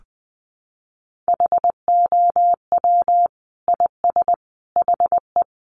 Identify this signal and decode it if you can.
Morse code, heard as a keyed tone, about 20 wpm, 700 Hz. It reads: HOW IS HE
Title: Sentences from Top 100 Words 20wpm